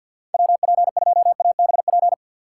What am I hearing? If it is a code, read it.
Morse code, heard as a keyed tone, about 52 wpm, 700 Hz. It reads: WP2ABP
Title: USA Call Signs - 6 Characters - 3x Character Spacing - 50wpm